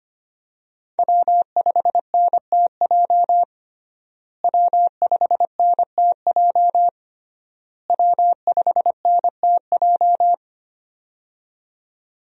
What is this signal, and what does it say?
Morse code, heard as a keyed tone, about 25 wpm, 700 Hz. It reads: W5NTJ W5NTJ W5NTJ
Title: USA Call Signs - 5 Characters - Repeated 3x - 25wpm